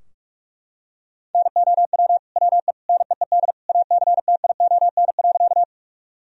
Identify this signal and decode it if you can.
Morse code, heard as a keyed tone, about 45 wpm, 700 Hz. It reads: NOW WE NEED ACTION.
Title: Sentences from Top 1000 Words - Review Part 4 - 45wpm